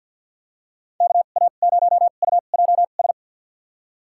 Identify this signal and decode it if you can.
Morse code, heard as a keyed tone, about 53 wpm, spaced as for 37 wpm, 690 Hz. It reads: KA0UJS
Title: USA Call Signs - 2x Character Spacing - 50wpm